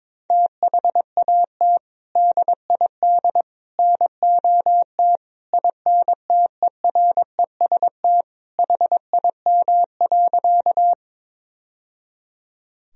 Morse code, heard as a keyed tone, about 21 wpm, 700 Hz. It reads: THAT DID NOT INTEREST HIM.
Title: Sentences from Top 1000 Words - Review Part 2 - 22wpm